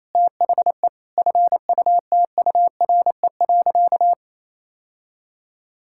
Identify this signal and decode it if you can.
Morse code, heard as a keyed tone, about 28 wpm, 700 Hz. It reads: THE FUTURE.
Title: Sentences from Top 1000 Words - Review Part 5 - 28wpm